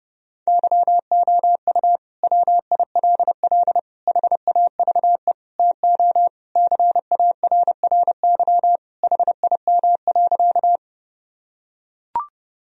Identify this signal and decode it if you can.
Morse code, heard as a keyed tone, about 30 wpm, 700 Hz. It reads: YOU WILL HAVE TO CARRY HIM.
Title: Sentences from Top 1000 Words - Review Part 1 - 30wpm